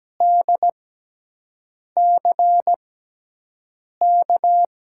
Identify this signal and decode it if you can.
Morse code, heard as a keyed tone, about 17 wpm, spaced as for 6 wpm, 700 Hz. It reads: DCK
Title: USA Call Signs - 6x Character Spacing - 17wpm